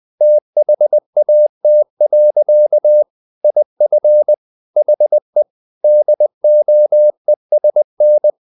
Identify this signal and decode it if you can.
Morse code, heard as a keyed tone, about 20 wpm, 600 Hz. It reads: THAT. IF HE DOESN
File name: the old man and the sea / the old man and the sea part 1 20wpm